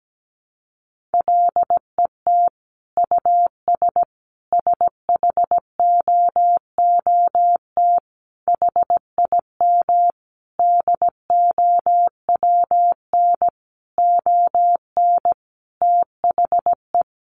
Morse code, heard as a keyed tone, about 17 wpm, 700 Hz. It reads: LET US SHOOT HIM DOWN ON THE
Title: Sentences from Top 1200 Words 17wpm